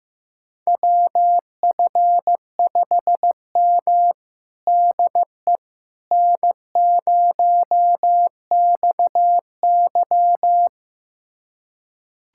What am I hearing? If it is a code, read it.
Morse code, heard as a keyed tone, about 15 wpm, 700 Hz. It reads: WF5M DE N0XY
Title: USA Callsign DE Callsign - 4 Characters - 15wpm